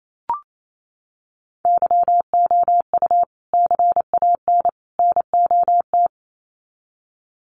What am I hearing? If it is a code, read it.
Morse code, heard as a keyed tone, about 28 wpm, 700 Hz. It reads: YOU CAN NOT
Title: Sets of 3 Words 28wpm